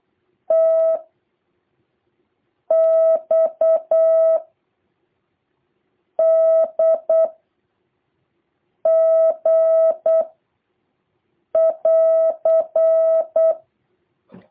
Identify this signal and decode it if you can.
Morse code, heard as a keyed tone, about 8 wpm, 650 Hz. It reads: T X D G <AR>
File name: jh8cbh / 2023cw